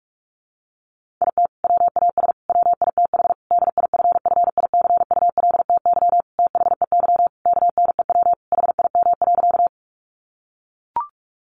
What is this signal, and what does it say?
Morse code, heard as a keyed tone, about 44 wpm, 700 Hz. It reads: IT WAS WITH DIFFICULTY THEY KNEW HIM.
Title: Sentences from Top 1500 Words 45wpm